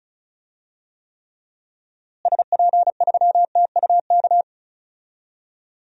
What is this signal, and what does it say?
Morse code, heard as a keyed tone, about 36 wpm, 700 Hz. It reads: SP3TUK
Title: DXCC-100 Call Signs - 6 Characters - Repeated 3x - 35wpm